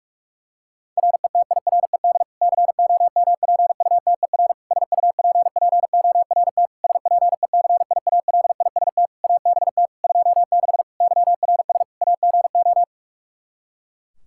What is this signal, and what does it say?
Morse code, heard as a keyed tone, about 46 wpm, 700 Hz. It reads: RETIRED COMPUTER SUPPORT SPECIALIST ABT 26 YRS AGO
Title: Practice QSO - Exchange 3 - More Info - 45wpm